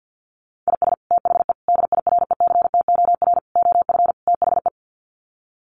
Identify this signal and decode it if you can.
Morse code, heard as a keyed tone, about 50 wpm, 700 Hz. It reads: IS THE DIRECTOR OF THE
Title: Sets of 5 Words Encore2 50wpm